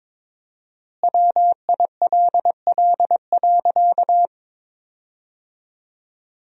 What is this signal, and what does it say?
Morse code, heard as a keyed tone, about 23 wpm, 700 Hz. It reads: WILL.
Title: 4x Word-Spacing - Sentences from Top 500 Words - 22wpm